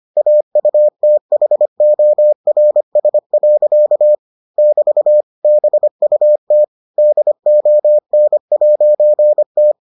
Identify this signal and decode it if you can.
Morse code, heard as a keyed tone, about 25 wpm, 600 Hz. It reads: AUTHORS. <BT> BUT DON'T